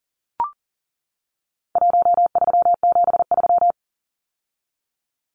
Morse code, heard as a keyed tone, about 40 wpm, 700 Hz. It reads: 1373
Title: Numbers - Contest Style - 4-Digit 40wpm